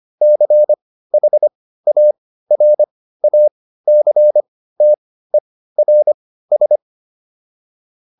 Morse code, heard as a keyed tone, about 25 wpm, spaced as for 15 wpm, 600 Hz. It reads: CHARACTERS